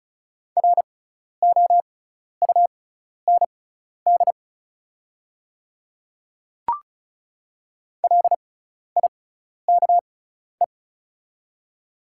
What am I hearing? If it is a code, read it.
Morse code, heard as a keyed tone, about 36 wpm, spaced as for 12 wpm, 690 Hz. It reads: ROUND LIKE
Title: 6x Character Spacing - Top 300 Words - 35wpm